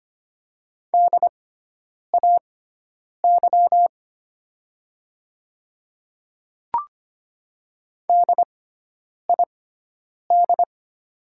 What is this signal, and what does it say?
Morse code, heard as a keyed tone, about 25 wpm, spaced as for 9 wpm, 700 Hz. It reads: DAY DID
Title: 6x Character Spacing - Top 200 Words - 25wpm